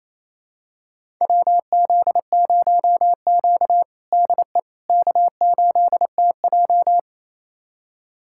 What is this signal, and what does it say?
Morse code, heard as a keyed tone, about 28 wpm, 700 Hz. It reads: WZ0Q DE K8TJ
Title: USA Callsign DE Callsign - 4 Characters - 28wpm